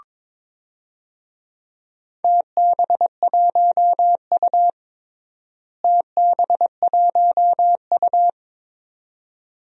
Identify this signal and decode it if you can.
Morse code, heard as a keyed tone, about 21 wpm, 700 Hz. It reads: TB1U TB1U
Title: DXCC-100 Call Signs - 4 Characters - Repeated 2x - 22wpm